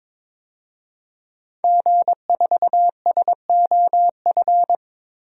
Morse code, heard as a keyed tone, about 22 wpm, 700 Hz. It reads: G4SOF
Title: DXCC-100 Call Signs - 5 Characters - Repeated 2x - 22wpm